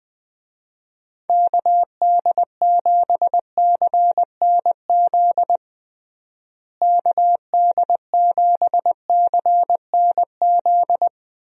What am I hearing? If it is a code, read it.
Morse code, heard as a keyed tone, about 21 wpm, 700 Hz. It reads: KD7CNZ KD7CNZ
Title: USA Call Signs - 6 Characters - Repeated 2x - 20wpm